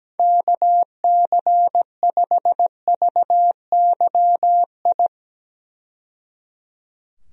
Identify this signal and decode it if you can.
Morse code, heard as a keyed tone, about 17 wpm, 700 Hz. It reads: KC5VYI
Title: USA Call Signs - 6 Characters - 17wpm